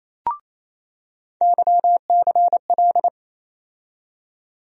Morse code, heard as a keyed tone, about 28 wpm, 700 Hz. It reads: YCL
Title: USA Call Sign Postfixes - 3 Letters 28wpm